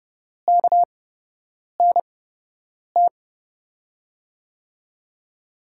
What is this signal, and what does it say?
Morse code, heard as a keyed tone, about 30 wpm, spaced as for 8 wpm, 700 Hz. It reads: KNT
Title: USA Call Signs - 6 Characters - 8x Character Spacing - 30wpm